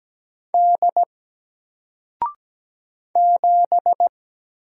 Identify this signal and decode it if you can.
Morse code, heard as a keyed tone, about 17 wpm, 700 Hz. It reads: D 7
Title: Single Letter-Number 17wpm